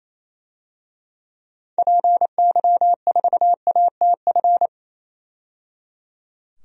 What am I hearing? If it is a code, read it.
Morse code, heard as a keyed tone, about 28 wpm, 700 Hz. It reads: PY4ATF
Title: DXCC-100 Call Signs - 6 Characters - 28wpm